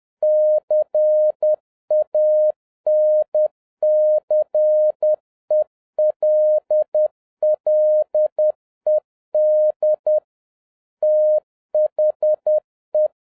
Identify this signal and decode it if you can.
Morse code, heard as a keyed tone, about 10 wpm, 610 Hz. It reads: CANCELLED THE